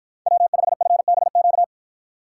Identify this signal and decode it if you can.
Morse code, heard as a keyed tone, about 51 wpm, 700 Hz. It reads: W5FBQ